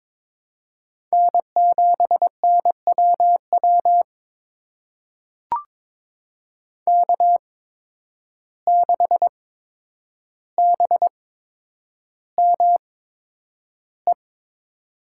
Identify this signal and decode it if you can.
Morse code, heard as a keyed tone, about 21 wpm, spaced as for 9 wpm, 700 Hz. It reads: N7NWW K6BME
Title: USA Call Signs - 5 Characters - 8x Character Spacing - 22wpm